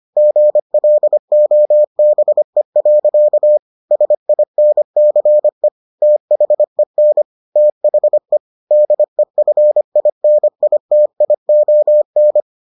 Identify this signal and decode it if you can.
Morse code, heard as a keyed tone, about 25 wpm, 600 Hz. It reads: GLOBE. SINCE THEN THE DEFINITION